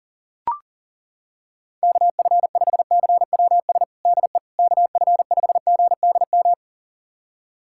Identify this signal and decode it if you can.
Morse code, heard as a keyed tone, about 40 wpm, 700 Hz. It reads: KF5CWS DE KF5GDM
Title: USA Callsign DE Callsign - 6 Characters - 40wpm